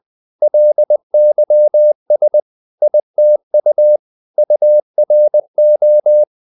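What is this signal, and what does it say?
Morse code, heard as a keyed tone, about 20 wpm, 600 Hz. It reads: LYS ITU URO
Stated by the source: Morse code practice of short Danish words at 20 wpm, from CW med Gnister.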